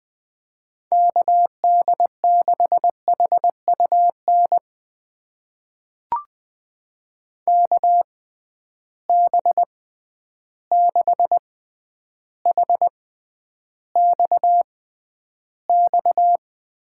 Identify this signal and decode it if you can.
Morse code, heard as a keyed tone, about 20 wpm, spaced as for 10 wpm, 700 Hz. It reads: KD6HUN KB6HXX
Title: USA Call Signs - 6 Characters - 6x Character Spacing - 20wpm